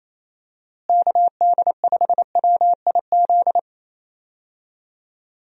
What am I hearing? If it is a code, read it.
Morse code, heard as a keyed tone, about 28 wpm, 700 Hz. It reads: KD5WIZ
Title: USA Call Signs - 6 Characters - Repeated 2x - 28wpm